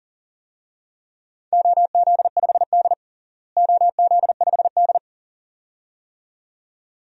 Morse code, heard as a keyed tone, about 40 wpm, 700 Hz. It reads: OZ5D OZ5D
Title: DXCC-100 Call Signs - 4 Characters - Repeated 2x - 40wpm